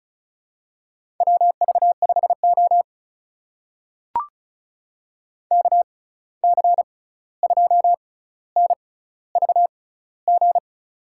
Morse code, heard as a keyed tone, about 36 wpm, spaced as for 16 wpm, 700 Hz. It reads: WV5O KC2NVG
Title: USA Call Signs - 6x Character Spacing - 35wpm